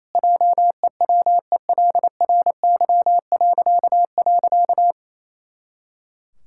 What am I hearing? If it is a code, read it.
Morse code, heard as a keyed tone, about 27 wpm, 700 Hz. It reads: JEWELRY..
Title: Pangram Challenge v2 - 28wpm